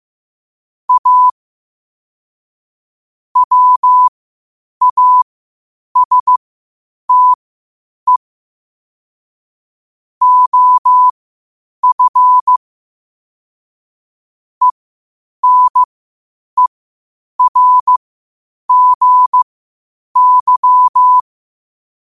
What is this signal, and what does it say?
Morse code, heard as a keyed tone, about 15 wpm, spaced as for 8 wpm, 1000 Hz. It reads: A WASTE OF ENERGY
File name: qotdfiles / 8wpm